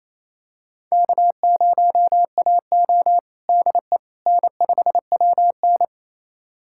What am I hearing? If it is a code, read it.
Morse code, heard as a keyed tone, about 28 wpm, 700 Hz. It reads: K0AO DE N5WN
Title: USA Callsign DE Callsign - 4 Characters - 28wpm